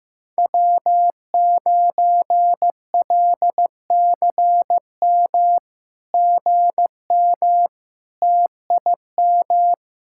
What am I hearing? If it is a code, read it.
Morse code, heard as a keyed tone, about 15 wpm, 700 Hz. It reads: W9LCM GM TIM
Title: SOTA QSO Practice 15wpm